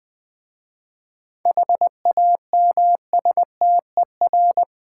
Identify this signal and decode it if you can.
Morse code, heard as a keyed tone, about 20 wpm, 700 Hz. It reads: HAMSTER
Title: Easy Words with Hints - 20wpm